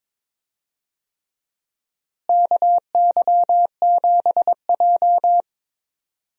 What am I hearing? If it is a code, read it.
Morse code, heard as a keyed tone, about 22 wpm, 700 Hz. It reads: KY7J
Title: USA Call Signs - 4 Characters - Repeated 3x - 22wpm